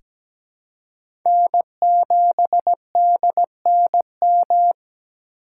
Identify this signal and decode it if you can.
Morse code, heard as a keyed tone, about 17 wpm, 700 Hz. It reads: N7DNM
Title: USA Call Signs - 5 Characters - 2x Character Spacing - 17wpm